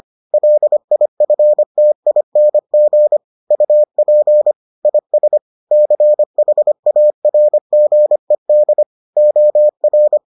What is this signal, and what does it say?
Morse code, heard as a keyed tone, about 25 wpm, 600 Hz. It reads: LIFTING UP IS CHARGED OR